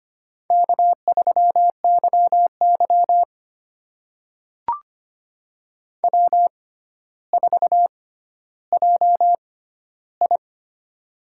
Morse code, heard as a keyed tone, about 25 wpm, spaced as for 13 wpm, 700 Hz. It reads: K3YY W4JI
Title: USA Call Signs - 4 Characters - 6x Character Spacing - 25wpm